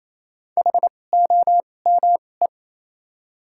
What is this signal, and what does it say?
Morse code, heard as a keyed tone, about 28 wpm, spaced as for 20 wpm, 700 Hz. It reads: HOME
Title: 2x Character Spacing - Top 300 Words - 28wpm